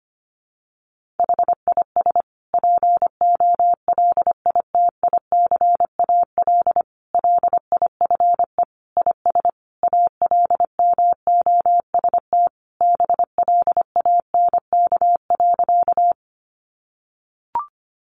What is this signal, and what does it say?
Morse code, heard as a keyed tone, about 25 wpm, 700 Hz. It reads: HIS POLITICAL LIFE IS ALMOST BLANK.
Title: Sentences from Top 1500 Words 25wpm